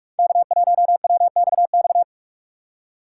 Morse code, heard as a keyed tone, about 46 wpm, 700 Hz. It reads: K1WXX
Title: USA Call Signs - 5 Characters - 4x Character Spacing - 45wpm